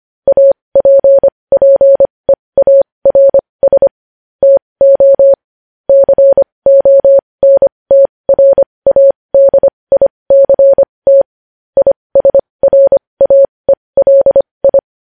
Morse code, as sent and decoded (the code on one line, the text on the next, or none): .- .--. .--. .- .-. ... / - --- / -.-. --- -. - .-. .- -.. .. -.-. - / .. ... .-. .- . .-.. ..
APPEARS TO CONTRADICT ISRAELI